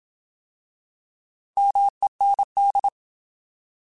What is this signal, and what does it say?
Morse code, heard as a keyed tone, about 27 wpm, 800 Hz. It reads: MEND
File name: Alphabetical / Alphabetical mp3s / Ms @25